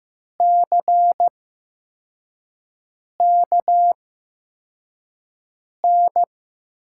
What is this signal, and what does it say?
Morse code, heard as a keyed tone, about 15 wpm, spaced as for 4 wpm, 700 Hz. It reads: CKN
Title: USA Call Signs - 5 Characters - 8x Character Spacing - 15wpm